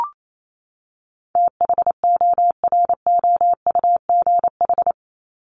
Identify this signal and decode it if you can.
Morse code, heard as a keyed tone, about 28 wpm, 700 Hz. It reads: THOROUGH
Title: LY Suffix - Top 100 Words with Base Words 28wpm